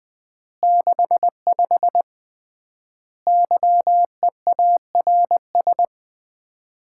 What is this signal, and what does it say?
Morse code, heard as a keyed tone, about 19 wpm, 700 Hz. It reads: 65 YEARS
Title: Common QSO Phrases - 3x Word-Spacing - 20wpm